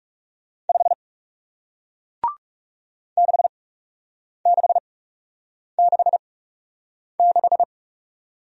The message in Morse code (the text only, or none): 5 6666